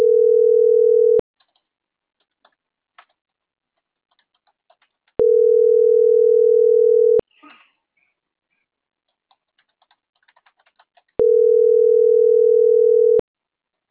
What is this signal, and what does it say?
Ringback tone (North America), 440 + 480 Hz.